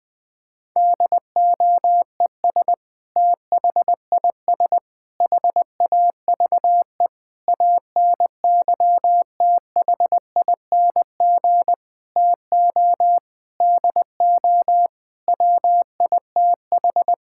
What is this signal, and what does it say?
Morse code, heard as a keyed tone, about 20 wpm, 700 Hz. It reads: DOES THIS HAVE ANYTHING TO DO WITH
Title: Sentences with 5-Word Phrases - 20wpm